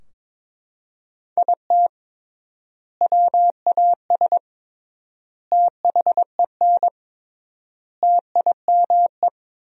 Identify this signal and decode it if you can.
Morse code, heard as a keyed tone, about 21 wpm, 700 Hz. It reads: IT WAS THEN TIME
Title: 3x Word-Spacing - Sentences from Top 300 Words - 22wpm